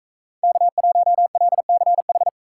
Morse code, heard as a keyed tone, about 44 wpm, 700 Hz. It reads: K1LCH